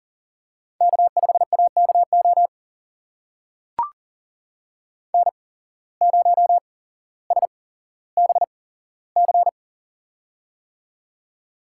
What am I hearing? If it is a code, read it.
Morse code, heard as a keyed tone, about 42 wpm, spaced as for 17 wpm, 700 Hz. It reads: K5AKO N0SBC
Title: USA Call Signs - 8x Character Spacing - 40wpm